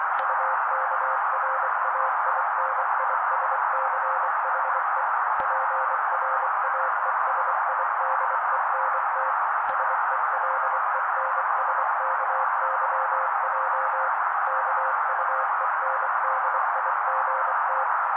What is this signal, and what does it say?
Morse code, heard as a keyed tone, about 24 wpm, 530 Hz. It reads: UKRAINISCHE PRAESIDENT SELENSKYJ KUENDIGT
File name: daily / 23wpm noise